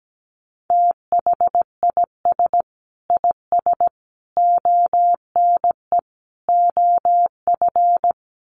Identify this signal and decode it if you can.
Morse code, heard as a keyed tone, about 17 wpm, 700 Hz. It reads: THIS IS ONE OF
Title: Sets of 4 Words 17wpm